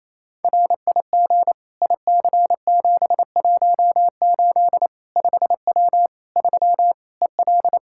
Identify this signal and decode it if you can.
Morse code, heard as a keyed tone, about 28 wpm, 700 Hz. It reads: RIG IC718 5W 3 EL
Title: K1USN SST QSO Practice 28wpm